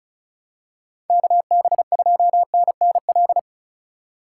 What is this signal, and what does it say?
Morse code, heard as a keyed tone, about 35 wpm, 700 Hz. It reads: KB2NNL
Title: USA Call Signs - 6 Characters - 3x Character Spacing - 35wpm